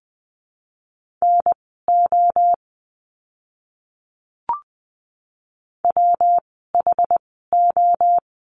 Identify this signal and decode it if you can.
Morse code, heard as a keyed tone, about 20 wpm, spaced as for 15 wpm, 700 Hz. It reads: NO WHO
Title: 115 Lesson 19f - Letter w - Words 20wpm